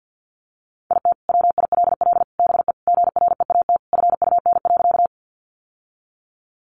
Morse code, heard as a keyed tone, about 50 wpm, 700 Hz. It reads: IT WILL BE GREAT FUN.
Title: Sentences from Top 700 Words 50wpm